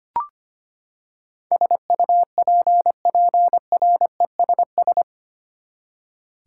Morse code, heard as a keyed tone, about 25 wpm, 700 Hz. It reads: SUPPRESS
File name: Eight Letter Words - 25wpm